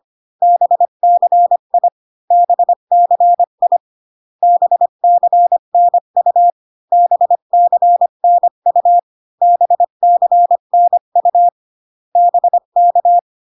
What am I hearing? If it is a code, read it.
Morse code, heard as a keyed tone, about 25 wpm, 700 Hz. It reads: BCI BCI BCNU BCNU BCNU BK